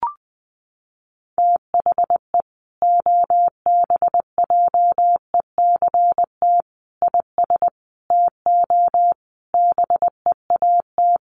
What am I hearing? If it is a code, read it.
Morse code, heard as a keyed tone, about 20 wpm, 700 Hz. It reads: THE OBJECT IS TO BEAT
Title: Sentences from Top 1300 Words 20wpm